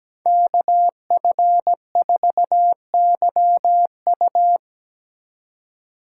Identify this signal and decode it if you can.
Morse code, heard as a keyed tone, about 17 wpm, 700 Hz. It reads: KF4YU
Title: Speed Racing - USA 2x2 Call Signs 17wpm